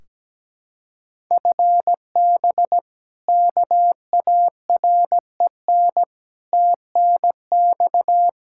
Morse code, heard as a keyed tone, about 17 wpm, 700 Hz. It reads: FB KAREN TNX
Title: K1USN SST QSO Practice 17wpm